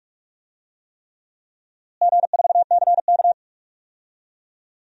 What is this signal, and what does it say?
Morse code, heard as a keyed tone, about 46 wpm, 700 Hz. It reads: G4CK
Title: DXCC-100 Call Signs - 4 Characters - Repeated 3x - 45wpm